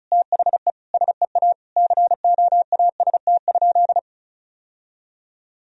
Morse code, heard as a keyed tone, about 36 wpm, 700 Hz. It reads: THE SEA COAST?
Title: Sentences from Top 1000 Words - Review Part 4 - 35wpm